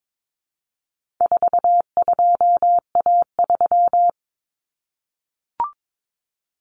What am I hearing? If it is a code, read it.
Morse code, heard as a keyed tone, about 22 wpm, 700 Hz. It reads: 42A3
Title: Common Cut Numbers - 4-Digit 22wpm